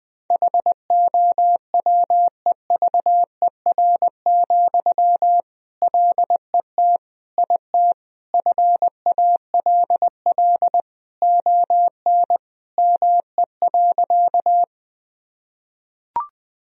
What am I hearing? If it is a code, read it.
Morse code, heard as a keyed tone, about 20 wpm, 700 Hz. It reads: HOWEVER, LET IT FALL ON ME.
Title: Sentences from Top 1000 Words - Review Part 2 - 20wpm